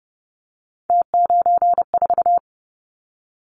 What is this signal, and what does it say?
Morse code, heard as a keyed tone, about 30 wpm, 700 Hz. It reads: T94
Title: Callsign Prefixes for Top 100 DXCC - 30wpm